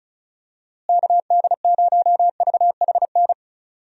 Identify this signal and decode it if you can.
Morse code, heard as a keyed tone, about 36 wpm, 700 Hz. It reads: KD0VHN
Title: USA Call Signs - 6 Characters - Repeated 2x - 35wpm